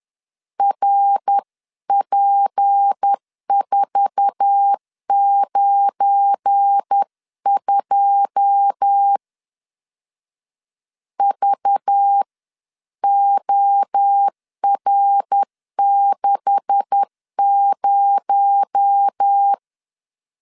Morse code, heard as a keyed tone, about 10 wpm, 800 Hz. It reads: RP492 VOR60
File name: CW / DET / 1 9 1500C S A 10wpm 4511